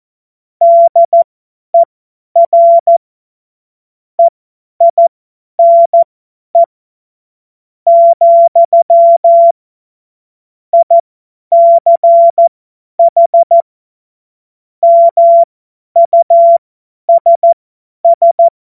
Morse code, heard as a keyed tone, about 14 wpm, spaced as for 9 wpm, 680 Hz. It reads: DER EINE, ICH MUSS
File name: Morsekurs / Witze / Witz 22. Februar 2022 50 F70